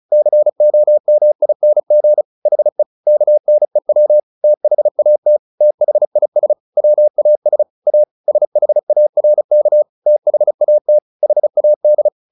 Morse code, heard as a keyed tone, about 34 wpm, 600 Hz. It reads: COMING HE KNEW THAT THIS WAS A SHARK THAT HAD
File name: the old man and the sea 2 35wpm